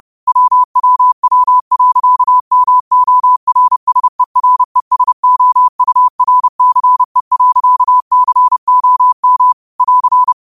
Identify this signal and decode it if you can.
Morse code, heard as a keyed tone, about 30 wpm, 1000 Hz. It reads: WWW.MORSERESOURCE.COM <AR>
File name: qotdfiles / 30wpm